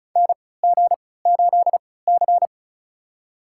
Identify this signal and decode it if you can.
Morse code, heard as a keyed tone, about 36 wpm, spaced as for 20 wpm, 700 Hz. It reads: NG8C
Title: USA Call Signs - 4 Characters - 3x Character Spacing - 35wpm